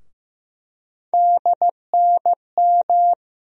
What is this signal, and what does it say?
Morse code, heard as a keyed tone, about 15 wpm, 700 Hz. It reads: DNM
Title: USA Call Sign Postfixes - 3 Letters 15wpm